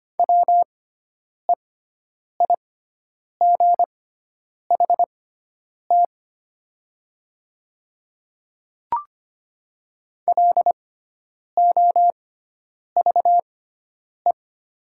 Morse code, heard as a keyed tone, about 25 wpm, spaced as for 9 wpm, 700 Hz. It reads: WEIGHT LOVE
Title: 6x Character Spacing - Top 500 Words - 25wpm